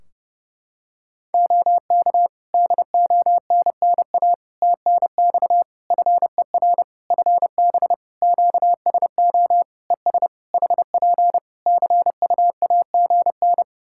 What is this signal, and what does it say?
Morse code, heard as a keyed tone, about 29 wpm, 700 Hz. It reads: OK DONNA TNX FER FB QSO ES HP CUAGN
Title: Practice QSO - Exchange 4 - Ending - 30wpm